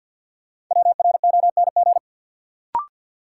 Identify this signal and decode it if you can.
Morse code, heard as a keyed tone, about 53 wpm, 700 Hz. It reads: WRONG
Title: Sentences from Top 900 Words - New Word Drill - 50wpm